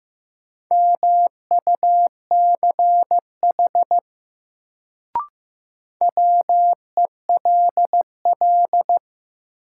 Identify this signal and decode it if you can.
Morse code, heard as a keyed tone, about 16 wpm, 700 Hz. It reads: MUCH WELL
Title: Sentences from Top 200 Words - New Word Drill - 15wpm